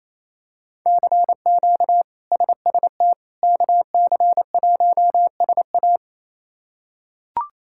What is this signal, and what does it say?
Morse code, heard as a keyed tone, about 28 wpm, 700 Hz. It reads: CQ SST KC1SA
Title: K1USN SST QSO Practice - 28wpm